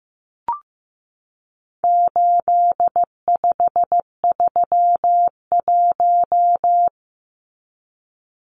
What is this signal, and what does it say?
Morse code, heard as a keyed tone, about 15 wpm, 700 Hz. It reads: 8531
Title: Numbers - 4-Digit 15wpm